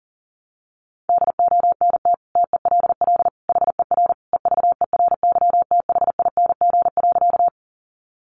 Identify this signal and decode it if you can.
Morse code, heard as a keyed tone, about 40 wpm, 700 Hz. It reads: DONT TELL HER EVERYTHING.
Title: Sentences from Top 700 Words 40wpm